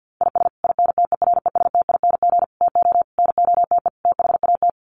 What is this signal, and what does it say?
Morse code, heard as a keyed tone, about 50 wpm, 700 Hz. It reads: IS INTERESTING TO NOTE THAT
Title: Sets of 5 Words Encore2 50wpm